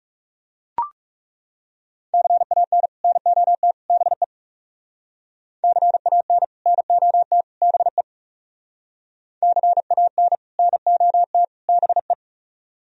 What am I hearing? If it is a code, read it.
Morse code, heard as a keyed tone, about 40 wpm, 700 Hz. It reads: CAN NOT BE CAN NOT BE CAN NOT BE